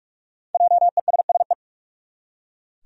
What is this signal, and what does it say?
Morse code, heard as a keyed tone, about 42 wpm, 700 Hz. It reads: JESSE
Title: USA Common Names - Max 5 Letters - 45wpm